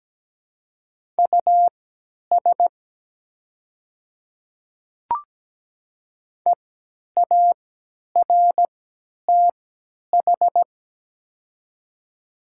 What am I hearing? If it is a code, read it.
Morse code, heard as a keyed tone, about 17 wpm, spaced as for 10 wpm, 700 Hz. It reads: US EARTH
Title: 3x Character Spacing - Top 200 Words - 17wpm